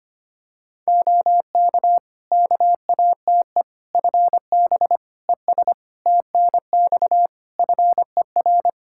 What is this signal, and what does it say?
Morse code, heard as a keyed tone, about 25 wpm, 700 Hz. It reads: OK KATE FB ES TNX FER
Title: Practice QSO - Exchange 2 - Info - 25wpm